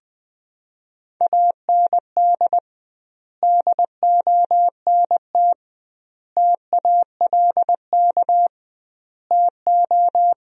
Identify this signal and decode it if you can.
Morse code, heard as a keyed tone, about 20 wpm, 700 Hz. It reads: AND DONT TALK TO
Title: 2x Word-Spacing - Sentences from Top 400 Words - 20wpm